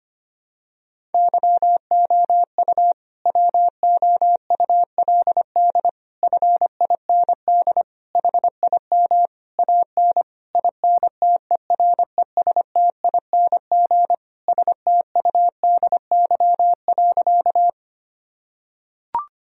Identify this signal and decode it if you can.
Morse code, heard as a keyed tone, about 25 wpm, 700 Hz. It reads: YOU WOULD FIND HIM AN INTERESTING STUDY.
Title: Sentences from Top 1000 Words - Review Part 1 - 25wpm